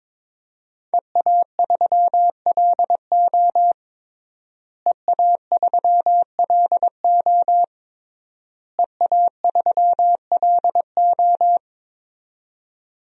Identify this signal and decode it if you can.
Morse code, heard as a keyed tone, about 22 wpm, 700 Hz. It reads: EA3LO EA3LO EA3LO
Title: DXCC-100 Call Signs - 5 Characters - Repeated 3x - 22wpm